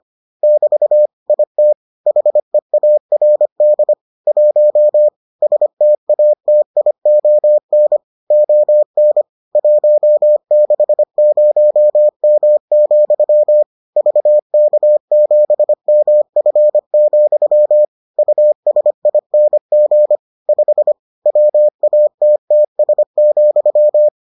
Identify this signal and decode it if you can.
Morse code, heard as a keyed tone, about 25 wpm, 600 Hz. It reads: <BT> IT HEARD 1 STATION ON 160M, VK7MF, USING 5 WATTS,